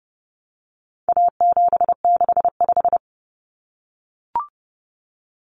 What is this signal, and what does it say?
Morse code, heard as a keyed tone, about 30 wpm, 700 Hz. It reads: A765
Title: Common Cut Numbers - 4-Digit 30wpm